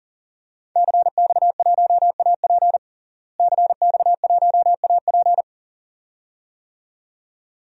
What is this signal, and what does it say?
Morse code, heard as a keyed tone, about 40 wpm, 700 Hz. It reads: CX1AP CX1AP
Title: DXCC-100 Call Signs - 5 Characters - Repeated 2x - 40wpm